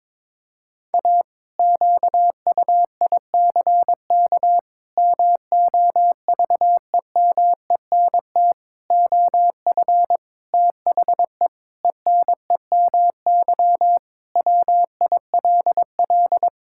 Morse code, heard as a keyed tone, about 22 wpm, 700 Hz. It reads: A QUICK MOVEMENT OF THE ENEMY WILL
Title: Pangram Challenge v2 - 22wpm